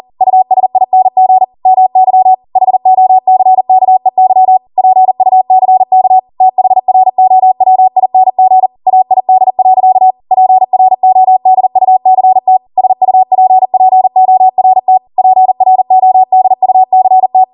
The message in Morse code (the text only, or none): USING MY HOCKEY PUCK THROWING AID. PRODUCT SUPPORT PRODUCT